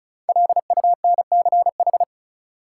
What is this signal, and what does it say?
Morse code, heard as a keyed tone, about 36 wpm, 700 Hz. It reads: LUNCH
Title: Five Letter Words - 35wpm